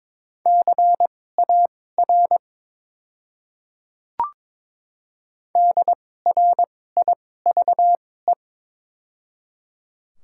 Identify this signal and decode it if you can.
Morse code, heard as a keyed tone, about 22 wpm, spaced as for 16 wpm, 690 Hz. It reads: CAR DRIVE
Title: 2x Character Spacing - Top 500 Words - 22wpm